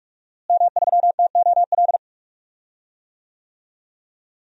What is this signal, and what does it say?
Morse code, heard as a keyed tone, about 43 wpm, 700 Hz. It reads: M3TOL
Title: DXCC-100 Call Signs - 5 Characters - Repeated 3x - 45wpm